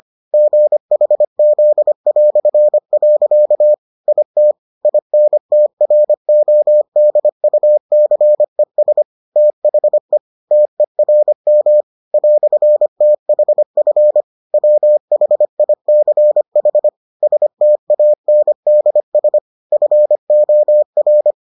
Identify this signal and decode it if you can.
Morse code, heard as a keyed tone, about 25 wpm, 600 Hz. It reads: GHZ". IT INTRODUCES THE TERM "THF WHICH STANDS FOR